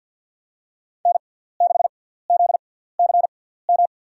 Morse code, heard as a keyed tone, about 50 wpm, spaced as for 17 wpm, 700 Hz. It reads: N6ZCM
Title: USA Call Signs - 6x Character Spacing - 50wpm